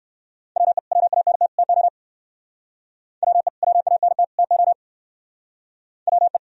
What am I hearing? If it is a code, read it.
Morse code, heard as a keyed tone, about 60 wpm, 700 Hz. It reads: WE WANT TO WE WANT TO WE